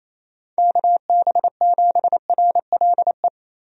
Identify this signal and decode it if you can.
Morse code, heard as a keyed tone, about 27 wpm, 700 Hz. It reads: KB7RLE
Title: USA Call Signs - 6 Characters - 28wpm